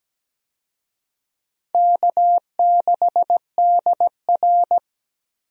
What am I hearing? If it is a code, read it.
Morse code, heard as a keyed tone, about 17 wpm, 700 Hz. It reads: K6DR
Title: USA Call Signs - 4 Characters - Repeated 2x - 17wpm